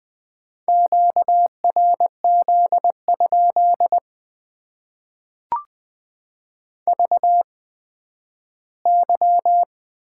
Morse code, heard as a keyed tone, about 20 wpm, spaced as for 12 wpm, 700 Hz. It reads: QRZ? VY
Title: Top 90 QSO Elements - 8x Character Spacing - 20wpm